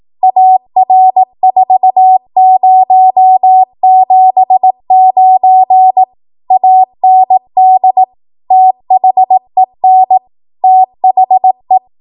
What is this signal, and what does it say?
Morse code, heard as a keyed tone, about 18 wpm, 750 Hz. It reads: AR4079 AND THEN THE
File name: Morse / 18WPM